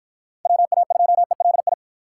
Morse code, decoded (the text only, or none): WA2ELI